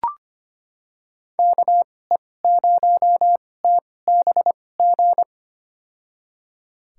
Morse code, heard as a keyed tone, about 25 wpm, spaced as for 18 wpm, 700 Hz. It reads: KE0TBG
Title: USA Call Signs - 6 Characters - 2x Character Spacing - 25wpm